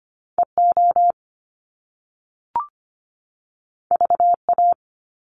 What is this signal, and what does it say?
Morse code, heard as a keyed tone, about 25 wpm, 700 Hz. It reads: EO VA
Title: Top Two Letter Combinations in Words 25wpm